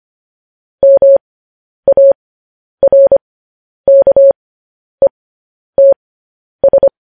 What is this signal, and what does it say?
Morse code, heard as a keyed tone, about 25 wpm, spaced as for 10 wpm, 550 Hz. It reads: MARKETS
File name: podcast / News headlines-1x-25x10